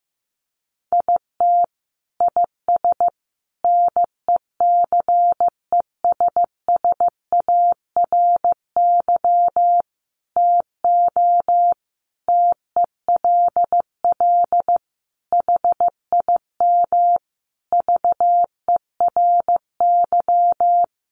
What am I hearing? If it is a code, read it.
Morse code, heard as a keyed tone, about 15 wpm, 700 Hz. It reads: IT IS NECESSARY TO TELL HIM VERY
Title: Sentences from Top 800 Words 15wpm